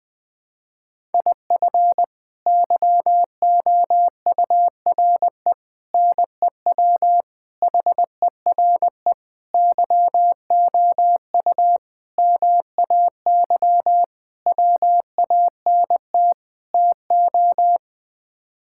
Morse code, heard as a keyed tone, about 20 wpm, 700 Hz. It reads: IF YOURE NEW HERE YOU MAY WANT TO
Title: Sets of 8 Words - 20wpm